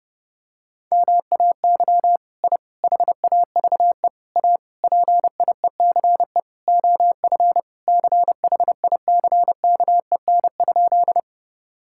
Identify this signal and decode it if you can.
Morse code, heard as a keyed tone, about 32 wpm, 700 Hz. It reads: MAY I HAVE A PIECE OF CHICKEN?
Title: Sentences from Top 1000 Words - Review Part 5 - 30wpm